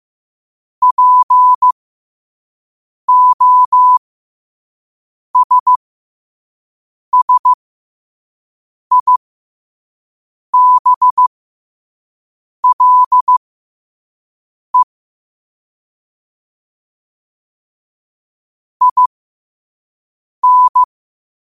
Morse code, heard as a keyed tone, about 15 wpm, spaced as for 5 wpm, 1000 Hz. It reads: POSSIBLE IN
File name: 5wpm